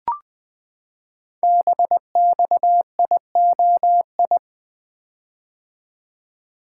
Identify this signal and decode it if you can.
Morse code, heard as a keyed tone, about 20 wpm, 700 Hz. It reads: BXIOI
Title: Code Groups - 5 Letters - 20wpm